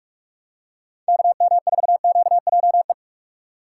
Morse code, heard as a keyed tone, about 42 wpm, 700 Hz. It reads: KM4QJE
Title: USA Call Signs - 6 Characters - 6x Character Spacing - 45wpm